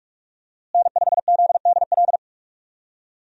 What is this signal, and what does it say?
Morse code, heard as a keyed tone, about 46 wpm, 700 Hz. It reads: N5ZDL
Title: USA Call Signs - 5 Characters - Repeated 3x - 45wpm